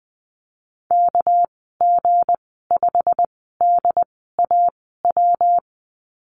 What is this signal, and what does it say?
Morse code, heard as a keyed tone, about 20 wpm, spaced as for 15 wpm, 700 Hz. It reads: KG5DAW
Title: 231 Lesson 37g - Letter x - Call Signs 20wpm